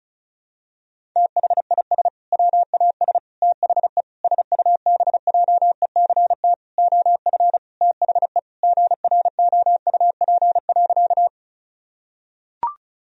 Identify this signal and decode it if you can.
Morse code, heard as a keyed tone, about 34 wpm, 700 Hz. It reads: THIS WAS THE SUBJECT OF THE GROUP.
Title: Sentences from Top 1000 Words - Review Part 1 - 35wpm